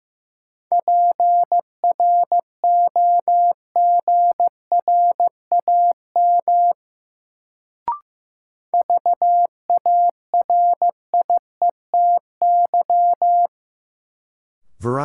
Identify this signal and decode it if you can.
Morse code, heard as a keyed tone, about 15 wpm, 700 Hz. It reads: PROGRAM VARIETY
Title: Sentences from Top 1000 Words - Review Part 4 - New Word Drill - 15wpm